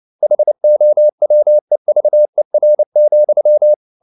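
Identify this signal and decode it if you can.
Morse code, heard as a keyed tone, about 29 wpm, 600 Hz. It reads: HOWEVER,
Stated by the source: Morse code practice at 28 wpm, text of Ken Follett's novel Never in English, from CW med Gnister.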